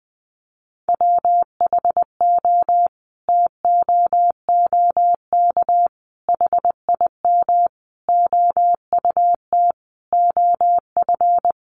Morse code, heard as a keyed tone, about 20 wpm, 700 Hz. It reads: WHO TOOK HIM OUT OF
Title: Sentences from Top 300 Words 20wpm